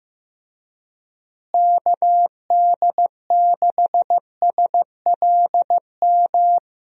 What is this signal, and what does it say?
Morse code, heard as a keyed tone, about 15 wpm, 700 Hz. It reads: KD6SLM
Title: USA Call Signs - 6 Characters - Repeated 2x - 15wpm